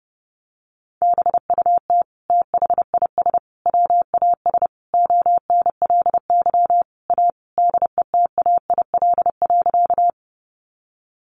Morse code, heard as a keyed tone, about 30 wpm, 700 Hz. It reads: BUT THIS WAS ONLY A DETAIL.
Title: Sentences from Top 1500 Words 30wpm